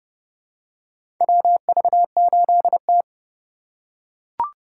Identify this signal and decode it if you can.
Morse code, heard as a keyed tone, about 30 wpm, 700 Hz. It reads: WV8T
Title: USA Call Signs - 4 Characters - 6x Character Spacing - 30wpm